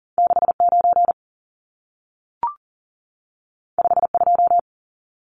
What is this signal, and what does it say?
Morse code, heard as a keyed tone, about 40 wpm, 700 Hz. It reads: 69 52
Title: Numbers - 2-Digit 40wpm